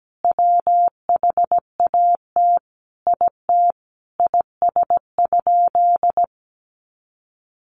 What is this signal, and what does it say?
Morse code, heard as a keyed tone, about 17 wpm, 700 Hz. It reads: WHAT IT IS?
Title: Sentences from Top 100 Words 17wpm